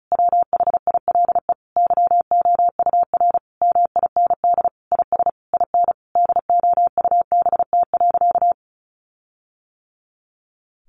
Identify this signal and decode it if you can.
Morse code, heard as a keyed tone, about 34 wpm, 700 Hz. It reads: WHILE YOUR MIND IS IN DOUBT.
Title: Sentences from Top 1500 Words 35wpm